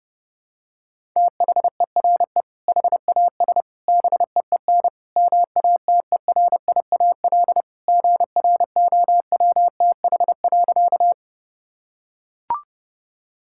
Morse code, heard as a keyed tone, about 32 wpm, 700 Hz. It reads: THERE HAS BEEN MATERIAL GROWTH.
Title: Sentences from Top 1000 Words - Review Part 5 - 30wpm